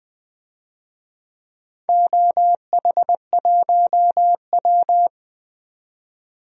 Morse code, heard as a keyed tone, about 20 wpm, 700 Hz. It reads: OH1W
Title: DXCC-100 Call Signs - 4 Characters - 20wpm